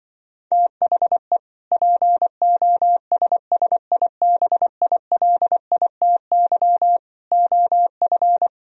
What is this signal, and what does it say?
Morse code, heard as a keyed tone, about 24 wpm, 700 Hz. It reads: THE POSSIBILITY OF
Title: Speed-Racing - Sets of 3 Words 20wpm